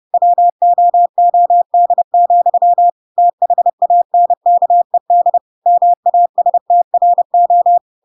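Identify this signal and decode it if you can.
Morse code, heard as a keyed tone, about 30 wpm, 700 Hz. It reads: WOOD, THANKED MASTRO
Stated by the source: Chapter 2 of The Adventures of Pinocchio read as a book in Morse code (CW) at 30 wpm, by ik1hge.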